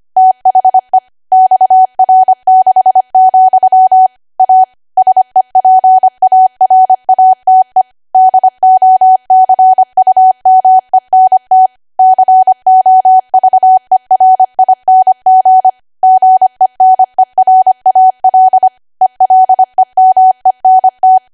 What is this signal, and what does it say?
Morse code, heard as a keyed tone, about 25 wpm, 750 Hz. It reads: THE XR6, A SEPARATE DOCUMENT COVERING GENERAL ELEMENT
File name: Archive / 25 WPM / 25WPM